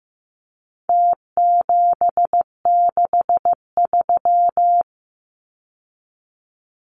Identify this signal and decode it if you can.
Morse code, heard as a keyed tone, about 15 wpm, 700 Hz. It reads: T763
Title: Numbers - Contest Style - 4-Digit 15wpm